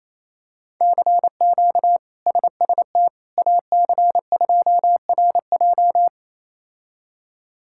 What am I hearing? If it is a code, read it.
Morse code, heard as a keyed tone, about 28 wpm, 700 Hz. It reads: CQ SST AC2RJ
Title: K1USN SST QSO Practice - 28wpm